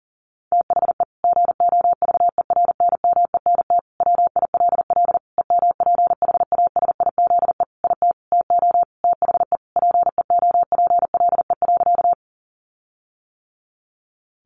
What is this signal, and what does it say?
Morse code, heard as a keyed tone, about 40 wpm, 700 Hz. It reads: THE GOVERNMENT WILL EMPHASIZE IT TO THE PEOPLE.
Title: Sentences from Top 1500 Words 40wpm